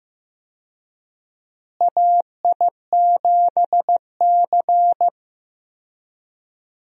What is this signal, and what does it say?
Morse code, heard as a keyed tone, about 15 wpm, 700 Hz. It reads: AI7C
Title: USA Call Signs - 4 Characters - Repeated 2x - 15wpm